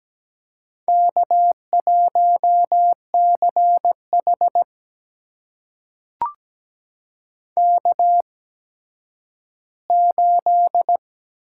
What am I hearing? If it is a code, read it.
Morse code, heard as a keyed tone, about 17 wpm, spaced as for 10 wpm, 700 Hz. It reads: K1CH K8
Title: USA Call Signs - 4 Characters - 8x Character Spacing - 17wpm